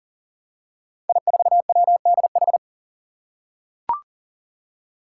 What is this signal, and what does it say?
Morse code, heard as a keyed tone, about 40 wpm, 700 Hz. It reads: I4WDH
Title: Five Challenging Letter-Number Combinations 40wpm